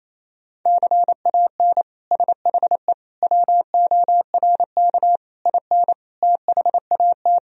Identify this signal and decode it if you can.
Morse code, heard as a keyed tone, about 28 wpm, 700 Hz. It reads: CAN SHE WORK IN THAT
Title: Sentences from Top 1000 Words - Review Part 4 - 28wpm